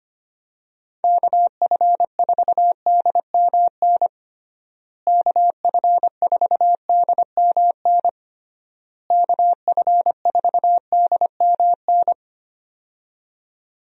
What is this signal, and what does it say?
Morse code, heard as a keyed tone, about 25 wpm, 700 Hz. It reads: KF4DMN KF4DMN KF4DMN
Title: USA Call Signs - 6 Characters - Repeated 3x - 25wpm